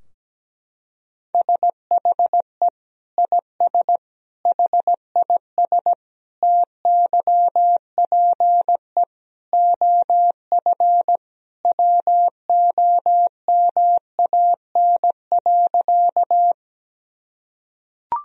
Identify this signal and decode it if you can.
Morse code, heard as a keyed tone, about 17 wpm, 700 Hz. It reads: SHE IS HIS TYPE OF WOMAN.
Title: Sentences from Top 1000 Words - Review Part 3 - 17wpm